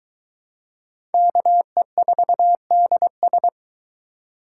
Morse code, heard as a keyed tone, about 23 wpm, 700 Hz. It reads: KE4DS